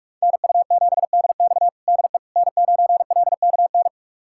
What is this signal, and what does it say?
Morse code, heard as a keyed tone, about 46 wpm, 700 Hz. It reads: NU7DX DE N9LKN
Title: USA Callsign DE Callsign - 5 Characters - 45wpm